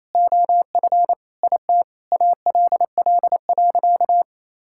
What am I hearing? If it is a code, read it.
Morse code, heard as a keyed tone, about 28 wpm, 700 Hz. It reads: OF IT ALL.
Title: Sentences from Top 1000 Words - Review Part 6 - 28wpm